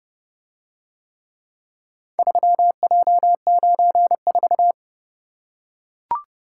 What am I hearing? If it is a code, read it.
Morse code, heard as a keyed tone, about 31 wpm, 700 Hz. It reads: 3J94